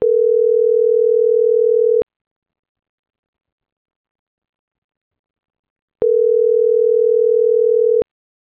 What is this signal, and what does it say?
Ringback tone (North America), 440 + 480 Hz.
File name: CENTRAL MABE / out luis.palacios - .77007